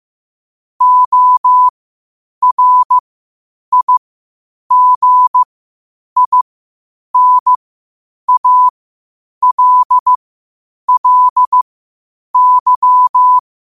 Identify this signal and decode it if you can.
Morse code, heard as a keyed tone, about 15 wpm, spaced as for 8 wpm, 1000 Hz. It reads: ORIGINALLY
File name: qotdfiles / 8wpm